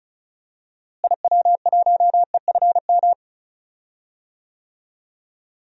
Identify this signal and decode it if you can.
Morse code, heard as a keyed tone, about 34 wpm, 700 Hz. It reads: IW1EFM
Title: DXCC-100 Call Signs - 6 Characters - Repeated 2x - 35wpm